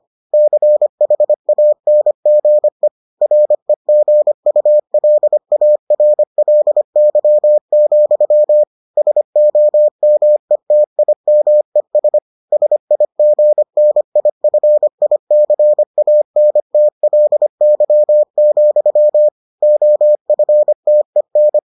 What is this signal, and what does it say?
Morse code, heard as a keyed tone, about 25 wpm, 600 Hz. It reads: CHANGE REGULARLY, SOMETIMES SIGNIFICANTLY, OFTEN